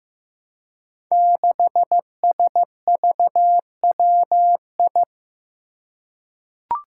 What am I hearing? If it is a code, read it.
Morse code, heard as a keyed tone, about 15 wpm, 700 Hz. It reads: 6SVWI
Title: Five Challenging Letter-Number Combinations 15wpm